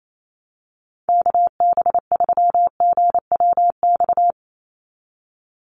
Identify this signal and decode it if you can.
Morse code, heard as a keyed tone, about 28 wpm, 700 Hz. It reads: KB3GWX
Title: USA Call Signs 28wpm